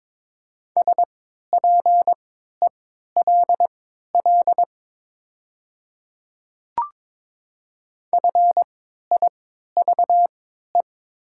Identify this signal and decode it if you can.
Morse code, heard as a keyed tone, about 22 wpm, spaced as for 13 wpm, 700 Hz. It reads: SPELL FIVE
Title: 3x Character Spacing - Top 500 Words - 22wpm